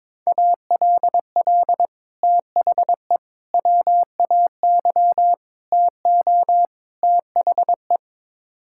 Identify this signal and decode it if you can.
Morse code, heard as a keyed tone, about 22 wpm, 700 Hz. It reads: ALL THE WAY TO THE
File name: practice / 22wpm / Build A Phrase - 5-Word Phrases - 22wpm